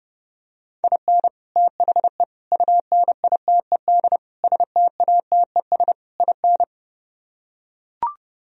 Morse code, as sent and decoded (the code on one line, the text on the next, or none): .. -. / - .... / ..- -. .. - . -.. / ... - .- - . ... / .. -.
IN THE UNITED STATES IN